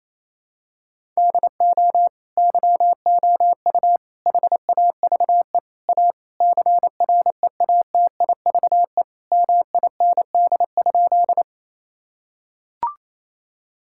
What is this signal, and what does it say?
Morse code, heard as a keyed tone, about 28 wpm, 700 Hz. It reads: DO YOU HAVE A CREATIVE MIND?
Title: Sentences from Top 1000 Words - Review Part 5 - 28wpm